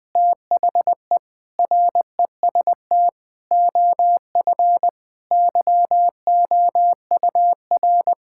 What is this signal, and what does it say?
Morse code, heard as a keyed tone, about 20 wpm, 700 Hz. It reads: THE REST OF YOUR
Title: Sentences with 5-Word Phrases - 20wpm